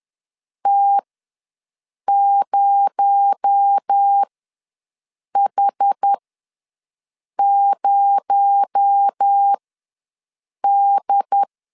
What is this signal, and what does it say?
Morse code, heard as a keyed tone, about 10 wpm, spaced as for 6 wpm, 800 Hz. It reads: T0H0D
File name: LEZIONI / 6 PKX0 / 6 PKX0 ws 10wpm 4220